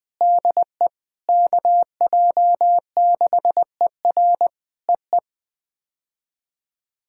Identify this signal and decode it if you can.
Morse code, heard as a keyed tone, about 20 wpm, 700 Hz. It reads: DE KJ6ER EE
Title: POTA QSO Practice - 20wpm